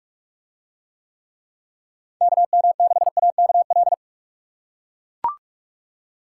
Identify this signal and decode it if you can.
Morse code, heard as a keyed tone, about 40 wpm, 700 Hz. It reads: KM6AKL